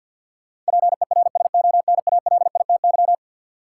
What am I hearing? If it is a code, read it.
Morse code, heard as a keyed tone, about 51 wpm, 700 Hz. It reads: PERSONALITY
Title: Sentences from Top 1400 Words - New Word Drill Alt - 50wpm